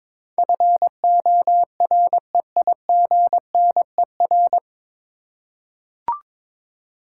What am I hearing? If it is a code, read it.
Morse code, heard as a keyed tone, about 22 wpm, 700 Hz. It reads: FOREIGNER
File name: 9 Letter Words - First 3 Letters Given - 22wpm